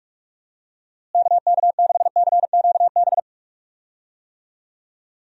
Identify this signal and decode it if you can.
Morse code, heard as a keyed tone, about 42 wpm, 700 Hz. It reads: KK6CQB
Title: USA Call Signs - 6 Characters - Repeated 2x - 45wpm